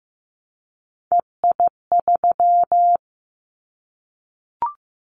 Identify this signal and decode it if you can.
Morse code, heard as a keyed tone, about 15 wpm, 700 Hz. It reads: EI3
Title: Callsign Prefixes for Top 50 DXCC - 15wpm